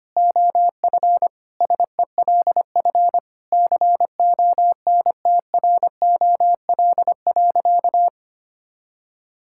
Morse code, heard as a keyed tone, about 25 wpm, 700 Hz. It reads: OF SELF CONTROL.
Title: Sentences from Top 1000 Words - Review Part 5 - 25wpm